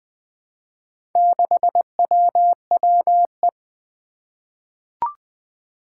Morse code, heard as a keyed tone, about 20 wpm, 700 Hz. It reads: 6WWE
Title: Four Challenging Letter-Number Combinations 20wpm